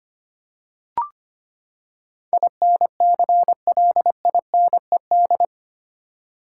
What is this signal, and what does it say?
Morse code, heard as a keyed tone, about 25 wpm, 700 Hz. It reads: INCLINED
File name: Eight Letter Words - 25wpm